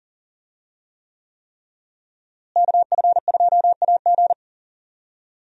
Morse code, heard as a keyed tone, about 38 wpm, 700 Hz. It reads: KF2AG
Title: USA Call Signs - 5 Characters - Repeated 3x - 40wpm